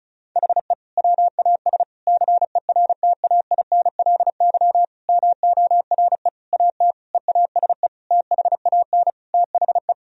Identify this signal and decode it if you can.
Morse code, heard as a keyed tone, about 34 wpm, 700 Hz. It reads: HE WAS CERTAINLY MORE AT EASE THAN THE
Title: Sentences from Top 1000 Words - Review Part 1 - 35wpm